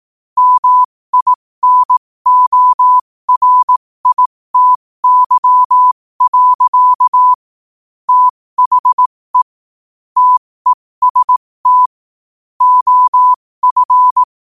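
Morse code, heard as a keyed tone, about 18 wpm, spaced as for 15 wpm, 1000 Hz. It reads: MINORITY. THE TEST OF
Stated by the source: Morse practice sent at 15 wpm.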